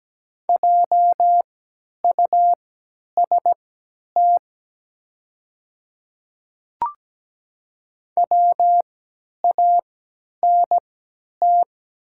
Morse code, heard as a keyed tone, about 17 wpm, spaced as for 10 wpm, 700 Hz. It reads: JUST WANT
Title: 3x Character Spacing - Top 200 Words - 17wpm